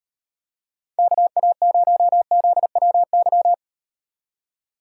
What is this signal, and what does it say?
Morse code, heard as a keyed tone, about 38 wpm, 700 Hz. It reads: KA0ZWY